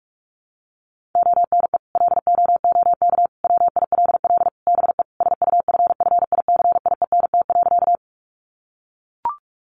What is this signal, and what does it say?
Morse code, heard as a keyed tone, about 44 wpm, 700 Hz. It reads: ONE LOOK WILL BE SUFFICIENT.
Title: Sentences from Top 1400 Words 45wpm